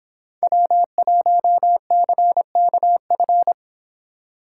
Morse code, heard as a keyed tone, about 26 wpm, 700 Hz. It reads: W1CKF